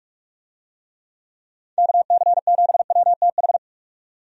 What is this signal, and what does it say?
Morse code, heard as a keyed tone, about 46 wpm, 700 Hz. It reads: KC7WTH
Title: USA Call Signs - 6 Characters - 45wpm